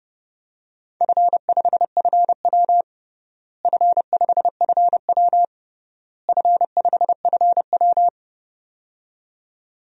Morse code, heard as a keyed tone, about 32 wpm, 700 Hz. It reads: F5FW F5FW F5FW
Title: DXCC-100 Call Signs - 4 Characters - Repeated 3x - 30wpm